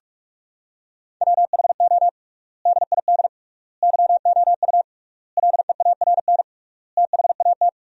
Morse code, heard as a keyed tone, about 46 wpm, 700 Hz. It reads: WHO DID YOU LEARN THAT
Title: 3x Word-Spacing - Sentences from Top 300 Words - 45wpm